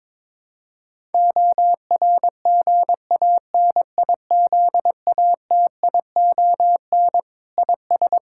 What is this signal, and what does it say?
Morse code, heard as a keyed tone, about 22 wpm, 700 Hz. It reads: ORGANIZATION IS
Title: Sentences from Top 1000 Words - Review Part 5 - 22wpm